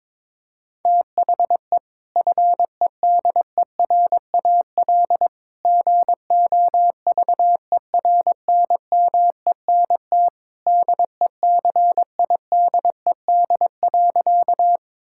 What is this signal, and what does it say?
Morse code, heard as a keyed tone, about 22 wpm, 700 Hz. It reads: THE FEDERAL GOVERNMENT DECIDED.
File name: practice / 22wpm / Sentences with 3-Word Phrases - 22wpm